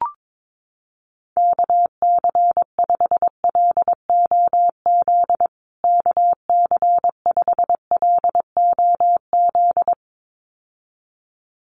Morse code, heard as a keyed tone, about 22 wpm, 700 Hz. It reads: KC5LOZ KC5LOZ
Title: USA Call Signs Repeated 2x 22wpm